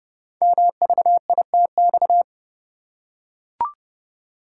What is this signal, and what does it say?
Morse code, heard as a keyed tone, about 30 wpm, 700 Hz. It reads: MVITX
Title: Code Groups - 5 Letters - 30wpm